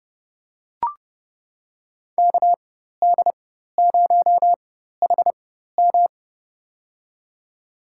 Morse code, heard as a keyed tone, about 32 wpm, spaced as for 14 wpm, 700 Hz. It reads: KD0HM
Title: USA Call Signs - 5 Characters - 4x Character Spacing - 30wpm